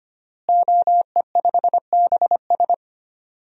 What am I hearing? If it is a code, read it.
Morse code, heard as a keyed tone, about 25 wpm, 700 Hz. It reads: OE5BS
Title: DXCC-100 Call Signs - 5 Characters - Repeated 3x - 25wpm